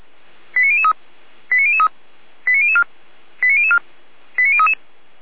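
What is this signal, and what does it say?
Five-tone selective call (ZVEI), heard as quick runs of five tones, tones of about 70 ms. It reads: 89003 89003 89004 89004 89033
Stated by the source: ZVEI five-tone call received on an SDR by feuerwehr-orlamuende.org, fire brigade Orlamünde.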